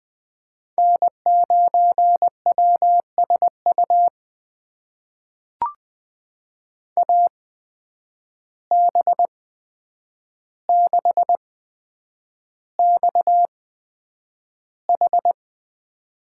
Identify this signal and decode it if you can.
Morse code, heard as a keyed tone, about 20 wpm, spaced as for 9 wpm, 700 Hz. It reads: N9WSU AB6XH
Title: USA Call Signs - 5 Characters - 8x Character Spacing - 20wpm